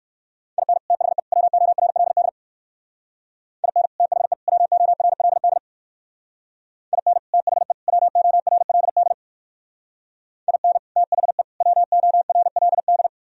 Morse code, heard as a keyed tone, about 59 wpm, 700 Hz. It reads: IN THE WORLD IN THE WORLD IN THE WORLD IN THE WORLD